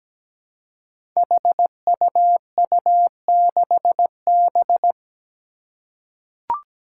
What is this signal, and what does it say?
Morse code, heard as a keyed tone, about 17 wpm, 700 Hz. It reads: HUU6B
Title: Code Groups - 5 Letter:Numbers - 17wpm